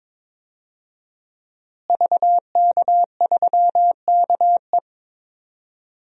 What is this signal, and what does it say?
Morse code, heard as a keyed tone, about 21 wpm, 700 Hz. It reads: VK3KE
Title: DXCC-100 Call Signs - 5 Characters - Repeated 3x - 22wpm